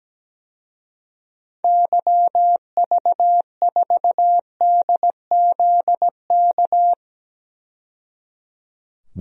Morse code, heard as a keyed tone, about 17 wpm, 700 Hz. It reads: YV4DZK
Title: DXCC-100 Call Signs - 6 Characters - 17wpm